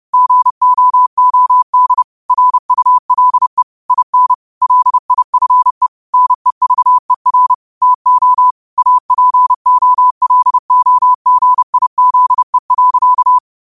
Morse code, heard as a keyed tone, about 30 wpm, 1000 Hz. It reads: GOOD RULE IN LIFE NEVER TO APOLOGIZE.
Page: New quotes every day in morse code at 30 Words per minute.